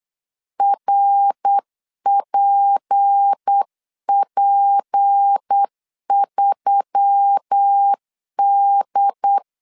Morse code, heard as a keyed tone, about 9 wpm, 800 Hz. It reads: RPP3D
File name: CW / DET / 1 6 1500C S C 8wpm 5219